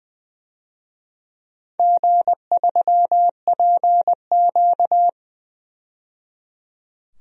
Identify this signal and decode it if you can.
Morse code, heard as a keyed tone, about 21 wpm, 700 Hz. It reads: G3PQ
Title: DXCC-100 Call Signs - 4 Characters - 20wpm